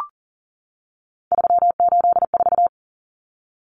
Morse code, heard as a keyed tone, about 40 wpm, 700 Hz. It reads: 384
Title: Common Cut Numbers - 3-Digit 40wpm